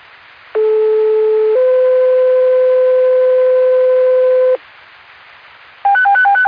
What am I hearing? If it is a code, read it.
Two-tone page, heard as a one-second tone, then a three-second tone, about 430 then 510 Hz. It is heard from 0.6 to 4.6 s.
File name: Lakeland EMS Gen 2026 04 24 04 37 14